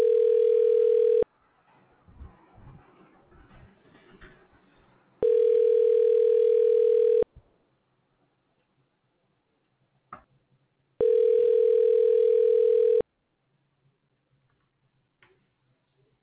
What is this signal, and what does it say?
Ringback tone (North America), 440 + 480 Hz.